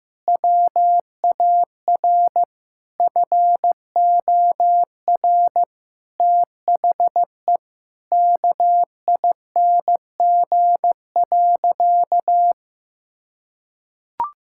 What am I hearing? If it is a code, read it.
Morse code, heard as a keyed tone, about 16 wpm, 700 Hz. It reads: WAR FOR THE KING.
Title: Sentences from Top 1000 Words - Review Part 6 - 15wpm